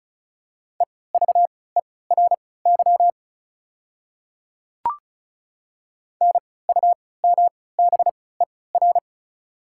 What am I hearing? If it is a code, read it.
Morse code, heard as a keyed tone, about 36 wpm, spaced as for 20 wpm, 690 Hz. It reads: EVERY NUMBER
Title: 3x Character Spacing - Top 300 Words - 35wpm